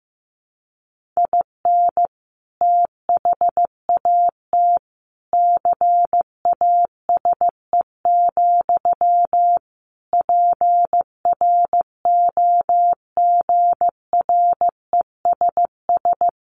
Morse code, heard as a keyed tone, about 15 wpm, 700 Hz. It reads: IN THAT CASE, PROGRESS
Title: Sentences from Top 1400 Words 15wpm